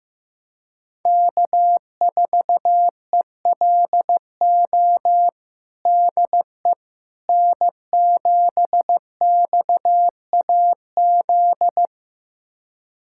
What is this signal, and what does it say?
Morse code, heard as a keyed tone, about 15 wpm, 700 Hz. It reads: K4ELO DE N7XAZ
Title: USA Callsign DE Callsign 15wpm